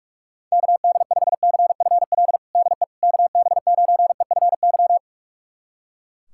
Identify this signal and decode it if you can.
Morse code, heard as a keyed tone, about 42 wpm, 700 Hz. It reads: KD5CFL DE KB9EFY
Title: USA Callsign DE Callsign 45wpm